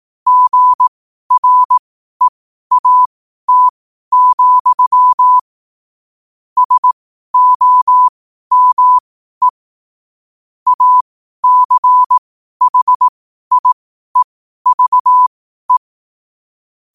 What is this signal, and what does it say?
Morse code, heard as a keyed tone, about 18 wpm, spaced as for 12 wpm, 1000 Hz. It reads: GREAT, SOME ACHIEVE
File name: qotdfiles / 12wpm